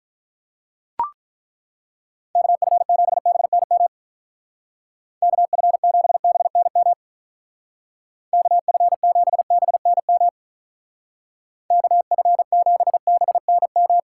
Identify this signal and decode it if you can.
Morse code, heard as a keyed tone, about 43 wpm, 700 Hz. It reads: KF7BNM KF7BNM KF7BNM KF7BNM
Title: Speed Racing - USA 2x3 Call Signs 35wpm